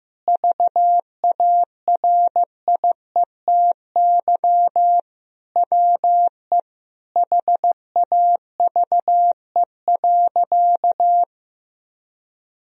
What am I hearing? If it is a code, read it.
Morse code, heard as a keyed tone, about 15 wpm, 700 Hz. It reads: VARIETY WE HAVE.
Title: Sentences from Top 1000 Words - Review Part 4 - 15wpm